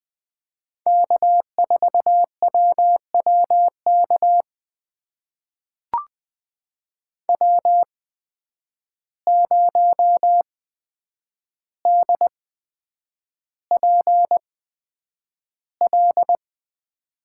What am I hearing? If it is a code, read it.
Morse code, heard as a keyed tone, about 20 wpm, spaced as for 9 wpm, 700 Hz. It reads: K4WWK W0DPL
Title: USA Call Signs - 5 Characters - 8x Character Spacing - 20wpm